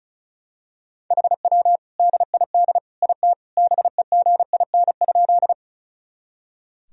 Morse code, heard as a keyed tone, about 36 wpm, 700 Hz. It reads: HW DID IT BEGIN?
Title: Sentences from Top 300 Words and Abbreviations 35wpm